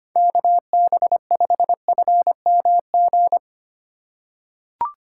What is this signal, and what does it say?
Morse code, heard as a keyed tone, about 25 wpm, 700 Hz. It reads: KB5FMG
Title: USA Call Signs - 6 Characters - 4x Character Spacing - 25wpm